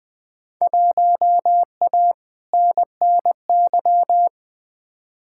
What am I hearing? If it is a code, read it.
Morse code, heard as a keyed tone, about 20 wpm, 700 Hz. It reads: 1A NNY
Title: Field Day Exchanges - 20wpm